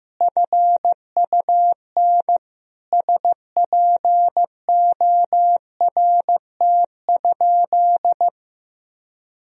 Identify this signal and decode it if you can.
Morse code, heard as a keyed tone, about 15 wpm, 700 Hz. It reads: FUN SPORT?
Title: Sentences from Top 1000 Words - Review Part 4 - 15wpm